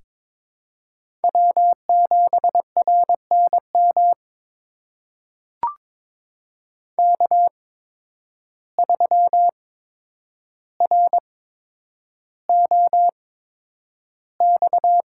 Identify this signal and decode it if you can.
Morse code, heard as a keyed tone, about 23 wpm, spaced as for 10 wpm, 700 Hz. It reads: W7RNM K3ROX
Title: USA Call Signs - 5 Characters - 8x Character Spacing - 22wpm